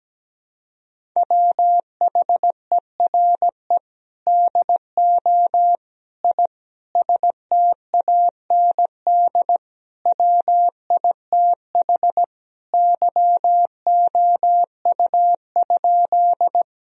Morse code, heard as a keyed tone, about 17 wpm, 700 Hz. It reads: WHERE DO I STAND WITH YOU?
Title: Sentences from Top 1000 Words - Review Part 1 - 17wpm